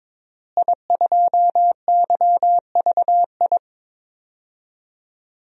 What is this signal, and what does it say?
Morse code, heard as a keyed tone, about 21 wpm, 700 Hz. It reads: I2YVI
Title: DXCC-100 Call Signs - 5 Characters - Repeated 3x - 22wpm